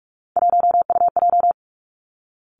Morse code, heard as a keyed tone, about 46 wpm, 700 Hz. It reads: 1UJ
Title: Three Challenging Letter-Number Combinations 45wpm